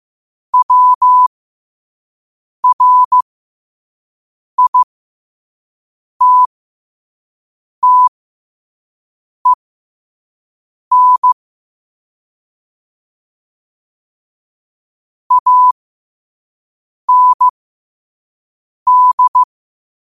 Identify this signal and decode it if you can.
Morse code, heard as a keyed tone, about 15 wpm, spaced as for 5 wpm, 1000 Hz. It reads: WRITTEN AND
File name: qotdfiles / 5wpm